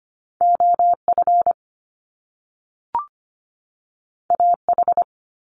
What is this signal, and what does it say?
Morse code, heard as a keyed tone, about 25 wpm, 700 Hz. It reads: OF AH
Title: Top Two Letter Combinations in Words 25wpm